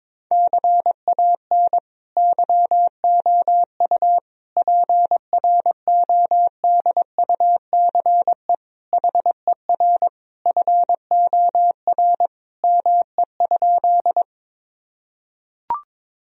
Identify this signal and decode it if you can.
Morse code, heard as a keyed tone, about 22 wpm, 700 Hz. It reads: CAN YOU PRODUCE HER FOR ME?
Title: Sentences from Top 1000 Words - Review Part 3 - 22wpm